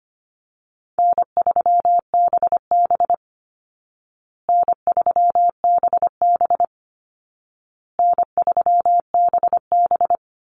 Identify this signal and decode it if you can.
Morse code, heard as a keyed tone, about 25 wpm, 700 Hz. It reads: N3BB N3BB N3BB
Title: Easy Call Signs Repeated 3x - 25wpm